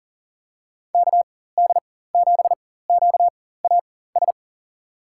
Morse code, heard as a keyed tone, about 40 wpm, spaced as for 19 wpm, 700 Hz. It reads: KD7QAS
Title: USA Call Signs - 6 Characters - 4x Character Spacing - 40wpm